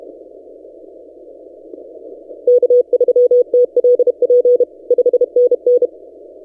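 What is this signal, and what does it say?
Morse code, heard as a keyed tone, about 32 wpm, 490 Hz. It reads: K3TLP 5NN
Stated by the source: Morse from DXpedition station A35X on 12 m, posted by N6ML.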